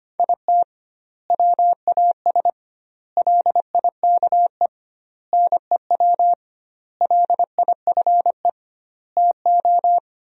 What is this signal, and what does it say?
Morse code, heard as a keyed tone, about 25 wpm, 700 Hz. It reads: IT WAS LIKE NEW LIFE TO
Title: 2x Word-Spacing - Sentences from Top 300 Words - 25wpm